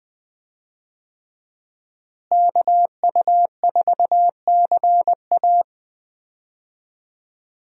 Morse code, heard as a keyed tone, about 20 wpm, 700 Hz. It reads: KU4CA
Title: USA Call Signs - 5 Characters - Repeated 3x - 20wpm